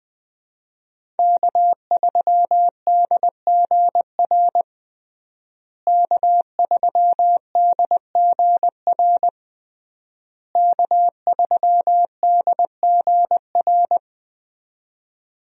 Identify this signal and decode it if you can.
Morse code, heard as a keyed tone, about 20 wpm, 700 Hz. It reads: K3DGR K3DGR K3DGR
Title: USA Call Signs - 5 Characters - Repeated 3x - 20wpm